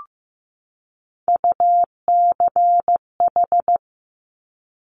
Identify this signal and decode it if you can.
Morse code, heard as a keyed tone, about 15 wpm, 700 Hz. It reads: UCH
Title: Top Three Letter Combinations in Words 15wpm